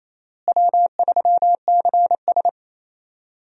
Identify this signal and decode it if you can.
Morse code, heard as a keyed tone, about 28 wpm, 700 Hz. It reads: W3CS
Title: USA Call Signs - 4 Characters - Repeated 2x - 28wpm